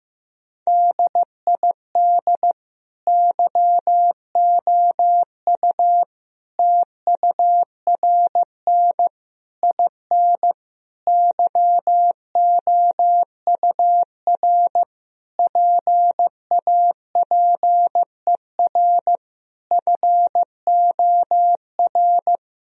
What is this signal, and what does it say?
Morse code, heard as a keyed tone, about 15 wpm, 700 Hz. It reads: DID YOU TURN IN YOUR PAPER FOR
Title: Sentences from Top 1000 Words - Review Part 2 - 15wpm